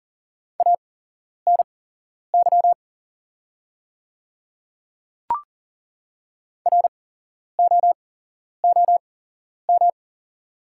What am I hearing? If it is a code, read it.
Morse code, heard as a keyed tone, about 41 wpm, spaced as for 11 wpm, 690 Hz. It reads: ANY ROOM
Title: 8x Character Spacing - Top 500 Words - 40wpm